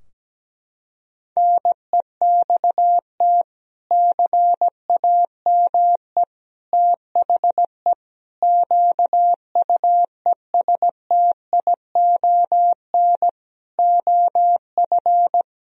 Morse code, heard as a keyed tone, about 17 wpm, 700 Hz. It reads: NEXT CAME THE QUESTION OF
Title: Sentences from Top 1000 Words - Review Part 4 - 17wpm